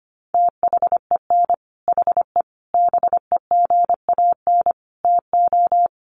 Morse code, heard as a keyed tone, about 25 wpm, 700 Hz. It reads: THEN HE BEGAN TO
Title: Sentences from Top 1200 Words 25wpm